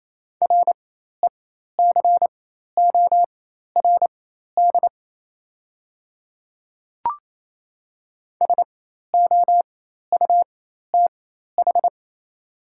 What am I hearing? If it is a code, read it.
Morse code, heard as a keyed tone, about 28 wpm, spaced as for 13 wpm, 690 Hz. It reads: RECORD SOUTH
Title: 4x Character Spacing - Top 500 Words - 28wpm